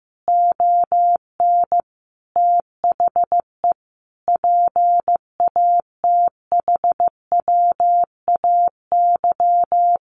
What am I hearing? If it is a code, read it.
Morse code, heard as a keyed tone, about 15 wpm, 700 Hz. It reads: ON THE PATHWAY
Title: A Sound Of Thunder 15wpm